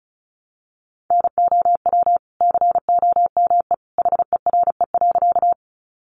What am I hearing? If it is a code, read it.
Morse code, heard as a keyed tone, about 34 wpm, 700 Hz. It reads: NOW COME HERE.
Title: Sentences from Top 200 Words 35wpm